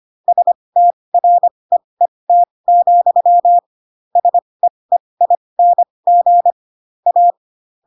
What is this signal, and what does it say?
Morse code, heard as a keyed tone, about 25 wpm, spaced as for 20 wpm, 700 Hz. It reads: STREET, SEEING A